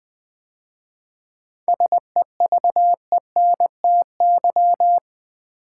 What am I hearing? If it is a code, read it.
Morse code, heard as a keyed tone, about 20 wpm, 700 Hz. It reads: SEVENTY
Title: 7 Letter Words - First 3 Letters Given - 20wpm